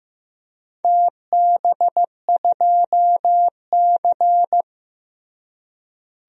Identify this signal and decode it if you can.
Morse code, heard as a keyed tone, about 15 wpm, 700 Hz. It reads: TB2C
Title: DXCC-100 Call Signs - 4 Characters - Repeated 2x - 15wpm